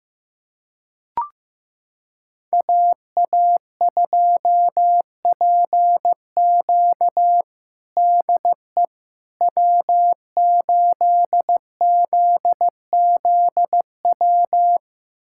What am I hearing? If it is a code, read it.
Morse code, heard as a keyed tone, about 15 wpm, 700 Hz. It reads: AA2PQ DE W8ZZW
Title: USA Callsign DE Callsign 15wpm